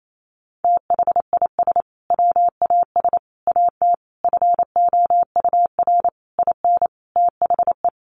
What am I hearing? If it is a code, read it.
Morse code, heard as a keyed tone, about 28 wpm, 700 Hz. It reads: THIS WAS AT FOUR IN THE
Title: Sentences from Top 1500 Words 28wpm